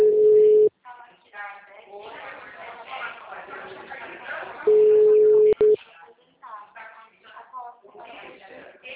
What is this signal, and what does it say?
Ringback tone (Europe, 425 Hz), with speech.